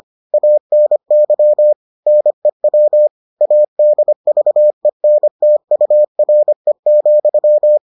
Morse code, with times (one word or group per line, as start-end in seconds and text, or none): 0.3-1.7 ANY
2.1-3.1 NEW
3.4-7.8 ADVENTURE,